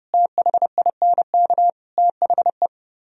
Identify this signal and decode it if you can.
Morse code, heard as a keyed tone, about 29 wpm, 700 Hz. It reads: THINK THE